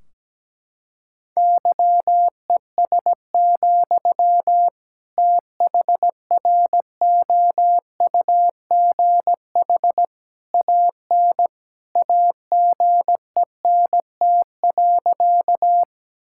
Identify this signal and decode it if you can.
Morse code, heard as a keyed tone, about 17 wpm, 700 Hz. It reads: YES, THROUGH AN AGENT.
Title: Sentences from Top 1600 Words 17wpm